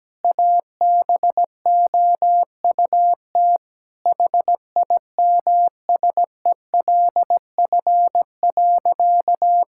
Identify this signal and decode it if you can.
Morse code, heard as a keyed tone, about 17 wpm, 700 Hz. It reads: ABOUT HIMSELF.
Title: Sentences from Top 1600 Words 17wpm